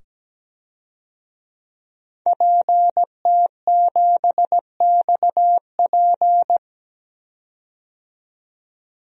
Morse code, heard as a keyed tone, about 17 wpm, 700 Hz. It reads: PT7XP
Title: DXCC-100 Call Signs - 5 Characters - 17wpm